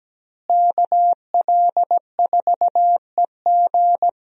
Morse code, heard as a keyed tone, about 17 wpm, 700 Hz. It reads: KL4EG
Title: USA Call Signs - 5 Characters - Repeated 2x - 17wpm